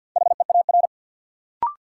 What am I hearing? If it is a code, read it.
Morse code, heard as a keyed tone, about 53 wpm, 700 Hz. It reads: HEAR
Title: Sentences from Top 300 Words - New Word Drill - 50wpm